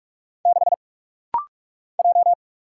Morse code, heard as a keyed tone, about 46 wpm, 700 Hz. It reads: 6 J
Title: Single Letter-Number - Rapid-Fire 45wpm